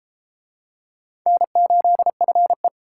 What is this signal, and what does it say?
Morse code, heard as a keyed tone, about 33 wpm, 700 Hz. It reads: N8FE